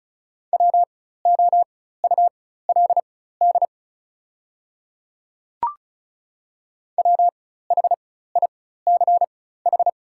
Morse code, heard as a keyed tone, about 36 wpm, spaced as for 16 wpm, 700 Hz. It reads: WOULD WHICH
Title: 4x Character Spacing - Top 200 Words - 35wpm